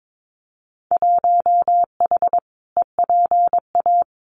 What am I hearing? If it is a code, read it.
Morse code, heard as a keyed tone, about 22 wpm, 700 Hz. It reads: 1H EPA
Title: Winter Field Day Exchanges 22wpm